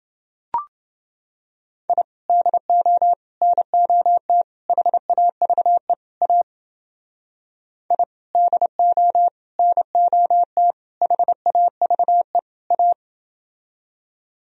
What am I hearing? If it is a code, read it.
Morse code, heard as a keyed tone, about 30 wpm, 700 Hz. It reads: I DO NOT HAVE A I DO NOT HAVE A